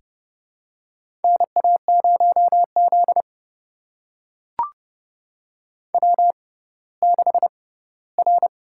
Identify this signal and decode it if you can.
Morse code, heard as a keyed tone, about 32 wpm, spaced as for 17 wpm, 700 Hz. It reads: NA0Z W6R
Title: USA Call Signs - 4 Characters - 6x Character Spacing - 30wpm